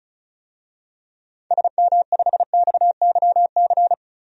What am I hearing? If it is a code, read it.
Morse code, heard as a keyed tone, about 34 wpm, 700 Hz. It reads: SM5XYC
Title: DXCC-100 Call Signs - 6 Characters - 35wpm